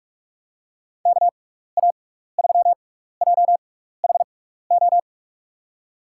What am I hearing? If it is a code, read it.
Morse code, heard as a keyed tone, about 46 wpm, 700 Hz. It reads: K A 3 J H O
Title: USA Call Signs - 6 Characters - 6x Character Spacing - 45wpm